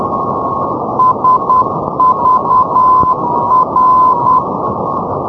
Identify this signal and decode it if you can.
Morse code, heard as a keyed tone, about 10 wpm, 1020 Hz, with radio noise. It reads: SVR
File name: europe / hng / hng-670 ndb sagvar svr-348+srb-78-tpl-348+hrv-240-sk-350-np-aut-poellau-2011-rb